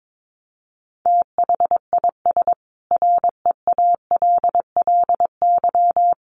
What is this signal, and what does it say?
Morse code, heard as a keyed tone, about 22 wpm, 700 Hz. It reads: THIS REALLY
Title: Sentences from Top 700 Words 22wpm